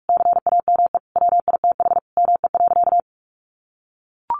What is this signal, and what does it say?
Morse code, heard as a keyed tone, about 44 wpm, 700 Hz. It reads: CAME WITH ME.
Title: Sentences from Top 1300 Words 45wpm